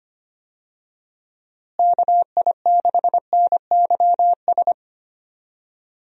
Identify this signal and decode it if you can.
Morse code, heard as a keyed tone, about 25 wpm, 700 Hz. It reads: KI6NYS
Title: USA Call Signs - 6 Characters - Repeated 3x - 25wpm